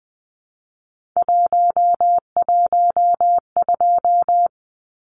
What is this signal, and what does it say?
Morse code, heard as a keyed tone, about 20 wpm, 700 Hz. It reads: 112
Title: Numbers - 3-Digit 20wpm